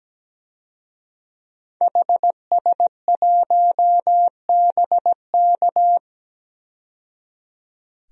Morse code, heard as a keyed tone, about 17 wpm, 700 Hz. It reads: HS1BK
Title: DXCC-100 Call Signs - 5 Characters - 17wpm